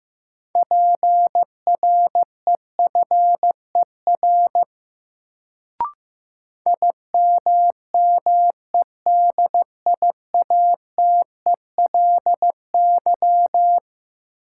Morse code, heard as a keyed tone, about 15 wpm, 700 Hz. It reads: PREFER IMMEDIATELY
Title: Sentences from Top 1100 Words - New Word Drill - 15wpm